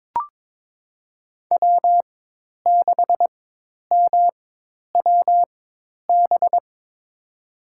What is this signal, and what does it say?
Morse code, heard as a keyed tone, about 22 wpm, spaced as for 10 wpm, 700 Hz. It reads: W6MWB
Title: USA Call Signs - 5 Characters - 4x Character Spacing - 22wpm